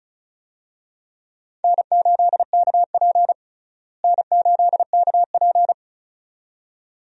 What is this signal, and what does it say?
Morse code, heard as a keyed tone, about 35 wpm, 700 Hz. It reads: N8KP N8KP
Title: USA Call Signs - 4 Characters - Repeated 2x - 35wpm